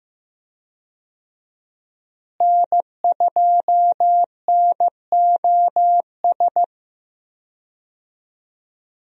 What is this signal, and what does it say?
Morse code, heard as a keyed tone, about 15 wpm, 700 Hz. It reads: N2NOS
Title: USA Call Signs - 5 Characters - Repeated 3x - 15wpm